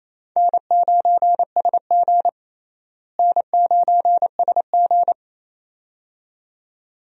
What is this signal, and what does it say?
Morse code, heard as a keyed tone, about 28 wpm, 700 Hz. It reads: N9SG N9SG
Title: USA Call Signs - 4 Characters - Repeated 2x - 28wpm